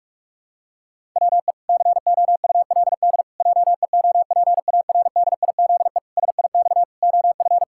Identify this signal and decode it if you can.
Morse code, heard as a keyed tone, about 41 wpm, 700 Hz. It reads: WE COULD JEOPARDIZE SIX OF
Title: Pangram Challenge v2 - 45wpm